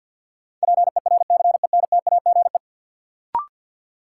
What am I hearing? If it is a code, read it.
Morse code, heard as a keyed tone, about 51 wpm, 700 Hz. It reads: PERCENTAGE
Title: Sentences from Top 1300 Words - New Word Drill Alt - 50wpm